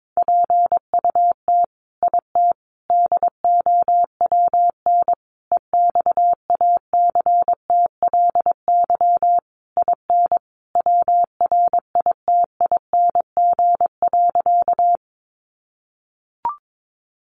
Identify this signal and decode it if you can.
Morse code, heard as a keyed tone, about 22 wpm, 700 Hz. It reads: PUT IT DOWN EXACTLY IN WRITING.
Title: Sentences from Top 900 Words 22wpm